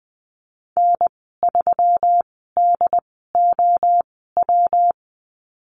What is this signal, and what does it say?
Morse code, heard as a keyed tone, about 20 wpm, spaced as for 14 wpm, 700 Hz. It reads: N3DOW
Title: 161 Lesson 26g - Letter g - Call Signs 20wpm